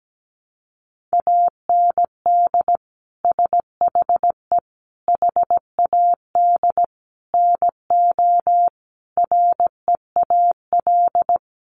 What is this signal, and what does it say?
Morse code, heard as a keyed tone, about 17 wpm, 700 Hz. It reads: AND SHE HAD NO REAL
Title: Sentences from Top 1200 Words 17wpm